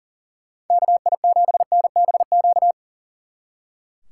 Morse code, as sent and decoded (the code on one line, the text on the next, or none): -.- .. --... -. -... --.-
KI7NBQ